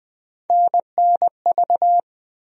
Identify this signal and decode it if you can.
Morse code, heard as a keyed tone, about 21 wpm, 700 Hz. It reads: NNV
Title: USA Call Sign Postfixes - 3 Letters 20wpm